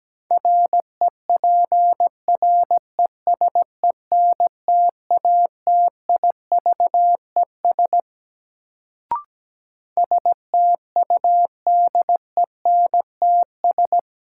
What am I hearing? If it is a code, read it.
Morse code, heard as a keyed tone, about 17 wpm, 700 Hz. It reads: REPRESENTATIVES STUDENTS
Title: Sets of 5 Words Encore - New Word Drill - 17wpm